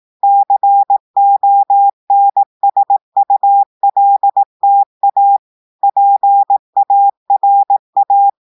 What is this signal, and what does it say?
Morse code, heard as a keyed tone, about 18 wpm, 800 Hz. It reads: CONSULTA PARA
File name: audios / portugais / portugais texte en clair 1 vitesse 18 mots minutes